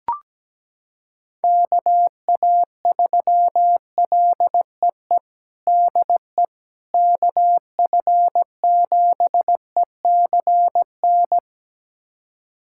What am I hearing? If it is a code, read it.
Morse code, heard as a keyed tone, about 17 wpm, 700 Hz. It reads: KA3LEE DE KF7ECN
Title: USA Callsign DE Callsign - 6 Characters - 17wpm